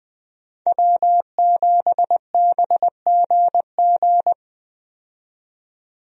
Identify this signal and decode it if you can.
Morse code, heard as a keyed tone, about 20 wpm, 700 Hz. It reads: W7BGG
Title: USA Call Signs - 5 Characters - 20wpm